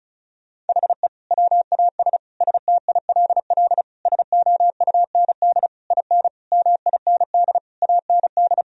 Morse code, heard as a keyed tone, about 36 wpm, 700 Hz. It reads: HE WAS STILL SOUND IN MIND AND
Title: Sentences from Top 1000 Words - Review Part 2 - 35wpm